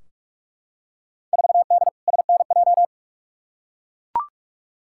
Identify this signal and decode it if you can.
Morse code, heard as a keyed tone, about 42 wpm, 700 Hz. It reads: 4D SNJ
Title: Field Day Exchanges Encore - 45wpm